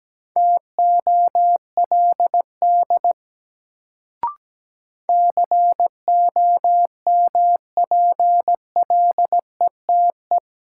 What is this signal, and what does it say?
Morse code, heard as a keyed tone, about 17 wpm, 700 Hz. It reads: TOLD COMPLETE
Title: Sentences from Top 400 Words - New Word Drill - 17wpm